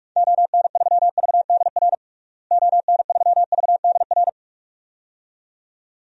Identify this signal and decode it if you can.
Morse code, heard as a keyed tone, about 46 wpm, 700 Hz. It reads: ON3VDR ON3VDR
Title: DXCC-100 Call Signs - 6 Characters - Repeated 2x - 45wpm